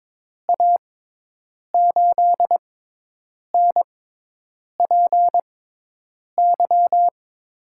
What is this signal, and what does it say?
Morse code, heard as a keyed tone, about 22 wpm, spaced as for 8 wpm, 700 Hz. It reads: A8NPY
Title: USA Call Signs - 6 Characters - 6x Character Spacing - 22wpm